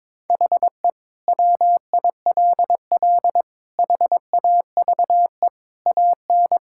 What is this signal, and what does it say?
Morse code, heard as a keyed tone, about 21 wpm, 700 Hz. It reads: HE WILL HAVE AN
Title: Sentences from Top 1000 Words - Review Part 5 - 22wpm